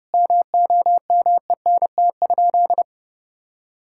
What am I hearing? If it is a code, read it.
Morse code, heard as a keyed tone, about 32 wpm, 700 Hz. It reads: MOMENT?
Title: Sentences from Top 1600 Words 30wpm